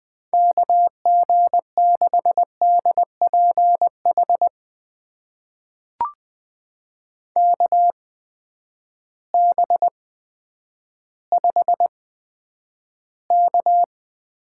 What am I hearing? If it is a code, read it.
Morse code, heard as a keyed tone, about 20 wpm, spaced as for 10 wpm, 700 Hz. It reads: KG6DPH KB5K
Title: USA Call Signs - 6 Characters - 8x Character Spacing - 20wpm